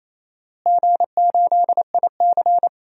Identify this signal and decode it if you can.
Morse code, heard as a keyed tone, about 28 wpm, 700 Hz. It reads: G8IC
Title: DXCC-100 Call Signs - 4 Characters - Repeated 2x - 28wpm